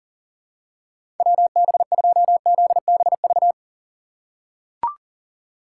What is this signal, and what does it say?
Morse code, heard as a keyed tone, about 40 wpm, 700 Hz. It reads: WB2ZBV
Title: USA Call Signs - 6x Character Spacing - 40wpm